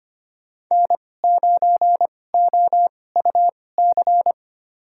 Morse code, heard as a keyed tone, about 25 wpm, spaced as for 18 wpm, 700 Hz. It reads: N9OUC
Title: USA Call Signs - 5 Characters - 2x Character Spacing - 25wpm